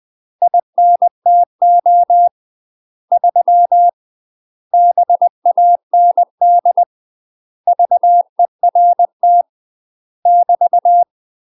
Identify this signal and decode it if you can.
Morse code, heard as a keyed tone, about 20 wpm, 700 Hz. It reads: INTO 3 BAND VERT <BT>
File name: UK MASTER QSOs TRAINING / 20WPM / UK MASTER QSOs 0064